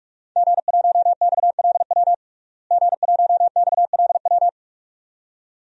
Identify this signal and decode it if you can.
Morse code, heard as a keyed tone, about 42 wpm, 700 Hz. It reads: G1XLW G1XLW
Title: DXCC-100 Call Signs - 5 Characters - Repeated 2x - 45wpm